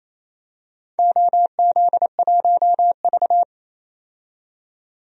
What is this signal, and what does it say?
Morse code, heard as a keyed tone, about 28 wpm, 700 Hz. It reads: OZ1V
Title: DXCC-100 Call Signs - 4 Characters - 28wpm